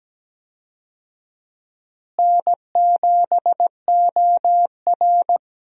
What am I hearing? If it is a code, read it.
Morse code, heard as a keyed tone, about 17 wpm, 700 Hz. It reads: N7OR
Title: USA Call Signs - 4 Characters - Repeated 3x - 17wpm